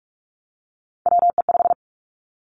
Morse code, heard as a keyed tone, about 46 wpm, 700 Hz. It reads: WE5
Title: Three Challenging Letter-Number Combinations 45wpm